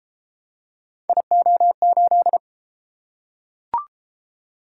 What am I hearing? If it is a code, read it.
Morse code, heard as a keyed tone, about 32 wpm, 700 Hz. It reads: IO8